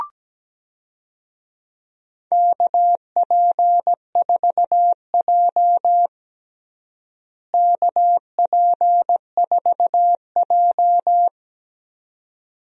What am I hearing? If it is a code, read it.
Morse code, heard as a keyed tone, about 17 wpm, 700 Hz. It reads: KP4J KP4J
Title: DXCC-100 Call Signs - 4 Characters - Repeated 2x - 17wpm